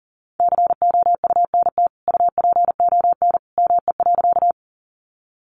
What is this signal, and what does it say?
Morse code, heard as a keyed tone, about 40 wpm, 700 Hz. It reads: COUNT UPON ME.
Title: Sentences from Top 1500 Words 40wpm